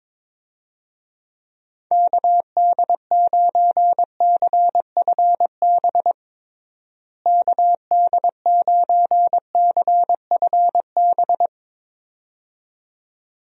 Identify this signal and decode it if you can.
Morse code, heard as a keyed tone, about 22 wpm, 700 Hz. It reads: KD9CFB KD9CFB
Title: USA Call Signs - 6 Characters - Repeated 2x - 22wpm